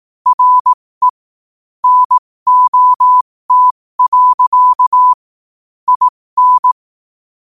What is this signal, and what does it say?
Morse code, heard as a keyed tone, about 18 wpm, spaced as for 15 wpm, 1000 Hz. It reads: RE NOT. IN
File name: qotdfiles / 15wpm